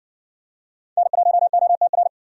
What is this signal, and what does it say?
Morse code, heard as a keyed tone, about 58 wpm, 700 Hz. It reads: N1OTR